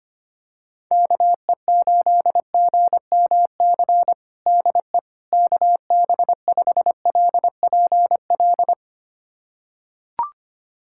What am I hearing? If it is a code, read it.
Morse code, heard as a keyed tone, about 25 wpm, 700 Hz. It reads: KE8GMC DE KB5LPL
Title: USA Callsign DE Callsign - 6 Characters - 25wpm